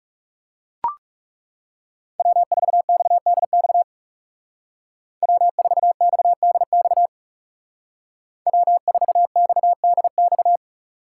Morse code, heard as a keyed tone, about 36 wpm, 700 Hz. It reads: W4XDX W4XDX W4XDX
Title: Speed Racing - USA 1x3 Call Signs 30wpm